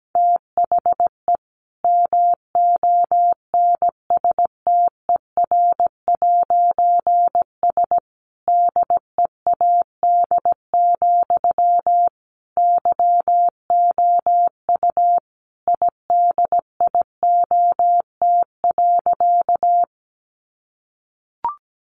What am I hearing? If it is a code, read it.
Morse code, heard as a keyed tone, about 17 wpm, 700 Hz. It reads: THE MONSTER'S DEAD, YOU IDIOT.
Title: A Sound Of Thunder 17wpm